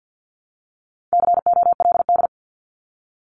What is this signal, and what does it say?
Morse code, heard as a keyed tone, about 50 wpm, 700 Hz. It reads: COLD
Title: Top 500 Words Repeated 50wpm